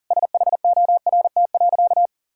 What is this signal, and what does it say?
Morse code, heard as a keyed tone, about 37 wpm, 700 Hz. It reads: SHORT.
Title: Sentences from Top 1600 Words 40wpm